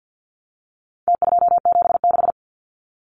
Morse code, heard as a keyed tone, about 50 wpm, 700 Hz. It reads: T276
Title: Numbers - Contest Style - 4-Digit 50wpm